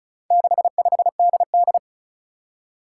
Morse code, heard as a keyed tone, about 36 wpm, 700 Hz. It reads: 65DD
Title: Four Challenging Letter-Number Combinations 35wpm